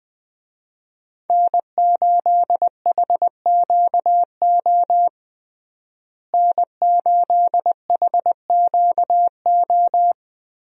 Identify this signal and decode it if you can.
Morse code, heard as a keyed tone, about 20 wpm, 700 Hz. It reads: N8HQO N8HQO
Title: USA Call Signs - 5 Characters - Repeated 3x - 20wpm